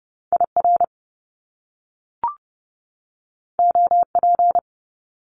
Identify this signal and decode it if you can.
Morse code, heard as a keyed tone, about 30 wpm, 700 Hz. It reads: IR OP
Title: Top Two Letter Combinations in Words 30wpm